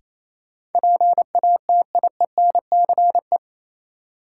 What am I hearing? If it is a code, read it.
Morse code, heard as a keyed tone, about 28 wpm, 700 Hz. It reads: PATIENCE
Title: Sentences from Top 1300 Words - New Word Drill Alt - 28wpm